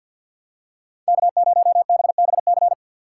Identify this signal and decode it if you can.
Morse code, heard as a keyed tone, about 51 wpm, 700 Hz. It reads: K0BBC
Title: USA Call Signs - 5 Characters - 6x Character Spacing - 50wpm